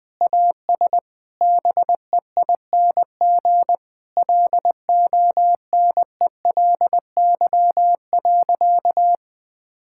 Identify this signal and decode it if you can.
Morse code, heard as a keyed tone, about 20 wpm, 700 Hz. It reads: AS BEING LONELY.
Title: Sentences with 6-Word Phrases - Spoken Beginning - 20wpm